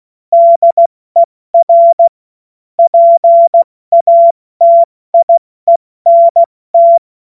Morse code, heard as a keyed tone, about 16 wpm, 680 Hz. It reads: DER PATIENT